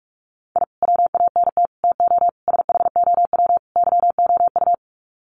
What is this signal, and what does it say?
Morse code, heard as a keyed tone, about 44 wpm, 700 Hz. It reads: I WANT TO SHOW YOU
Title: Sets of 5 Words 45wpm